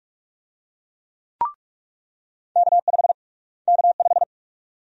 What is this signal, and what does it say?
Morse code, heard as a keyed tone, about 42 wpm, 700 Hz. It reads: K5 K5
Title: USA Call Sign Prefixes - Repeated 2x - 45wpm